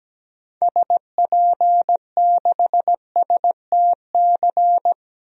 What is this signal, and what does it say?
Morse code, heard as a keyed tone, about 17 wpm, 700 Hz. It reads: SP6STC
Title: DXCC-100 Call Signs - 6 Characters - Repeated 3x - 17wpm